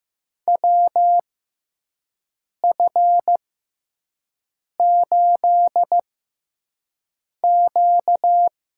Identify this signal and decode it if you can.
Morse code, heard as a keyed tone, about 15 wpm, spaced as for 5 wpm, 700 Hz. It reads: WF8Q
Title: USA Call Signs - 4 Characters - 6x Character Spacing - 15wpm